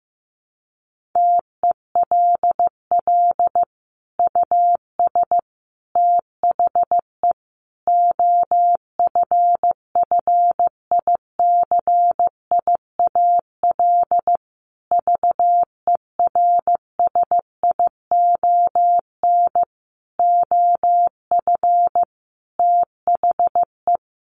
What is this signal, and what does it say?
Morse code, heard as a keyed tone, about 15 wpm, 700 Hz. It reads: TELL US THE OFFICIAL VERSION OF THE
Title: Sentences from Top 1400 Words 15wpm